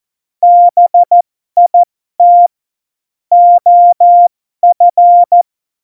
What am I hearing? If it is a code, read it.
Morse code, heard as a keyed tone, about 14 wpm, spaced as for 12 wpm, 700 Hz. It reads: BIT OF
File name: Morsekurs / Witze